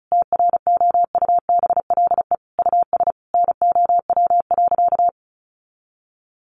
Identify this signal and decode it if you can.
Morse code, heard as a keyed tone, about 34 wpm, 700 Hz. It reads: TROUBLE US NOW.
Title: Sentences from Top 1400 Words 35wpm